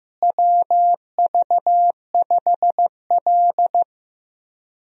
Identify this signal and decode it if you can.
Morse code, heard as a keyed tone, about 15 wpm, 700 Hz. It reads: WV5L
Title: USA Call Signs - 4 Characters - Repeated 2x - 15wpm